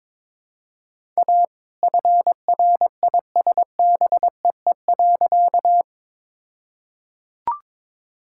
Morse code, heard as a keyed tone, about 21 wpm, 700 Hz. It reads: A FRISBEE.
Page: morse-code-ninja-practice-22wpm